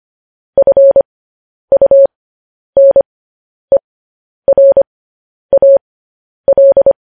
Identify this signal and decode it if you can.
Morse code, heard as a keyed tone, about 25 wpm, spaced as for 10 wpm, 550 Hz. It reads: FUNERAL
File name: podcast / News headlines-1x-25x10